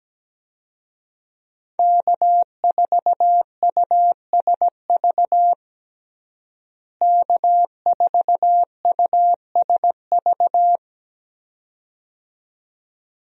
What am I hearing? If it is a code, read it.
Morse code, heard as a keyed tone, about 17 wpm, 700 Hz. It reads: K4USV K4USV
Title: USA Call Signs - 5 Characters - Repeated 2x - 17wpm